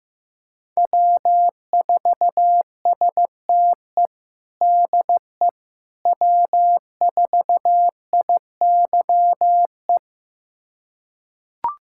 Morse code, heard as a keyed tone, about 15 wpm, 700 Hz. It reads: W4STE DE W4IYE
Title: USA Callsign DE Callsign - 5 Characters - 15wpm